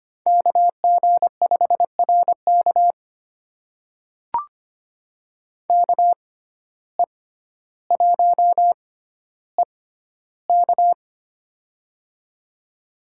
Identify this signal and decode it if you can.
Morse code, heard as a keyed tone, about 25 wpm, spaced as for 13 wpm, 700 Hz. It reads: KG5RK KE1EK
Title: USA Call Signs - 5 Characters - 6x Character Spacing - 25wpm